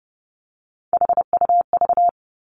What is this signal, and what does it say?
Morse code, heard as a keyed tone, about 30 wpm, 700 Hz. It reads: HUV
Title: Three Challenging Letter-Number Combinations 30wpm